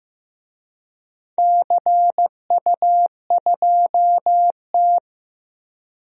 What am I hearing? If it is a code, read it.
Morse code, heard as a keyed tone, about 16 wpm, 700 Hz. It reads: CU2T
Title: DXCC-100 Call Signs - 4 Characters - Repeated 3x - 15wpm